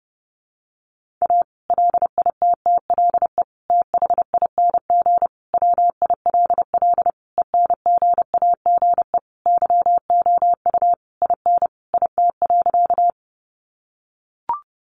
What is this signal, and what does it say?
Morse code, heard as a keyed tone, about 30 wpm, 700 Hz. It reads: A LITTLE THING WILL ENGAGE YOU IN IT.
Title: Sentences from Top 1500 Words 30wpm